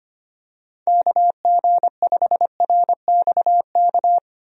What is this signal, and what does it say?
Morse code, heard as a keyed tone, about 25 wpm, 700 Hz. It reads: KG5RXK
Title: USA Call Signs - 6 Characters - 25wpm